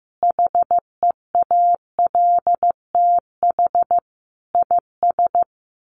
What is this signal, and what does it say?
Morse code, heard as a keyed tone, about 15 wpm, 700 Hz. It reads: HEALTH IS
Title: Sentences from Top 1200 Words 15wpm